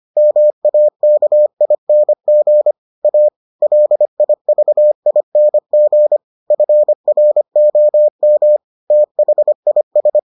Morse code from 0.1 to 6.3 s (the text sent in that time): MAKING A LIVING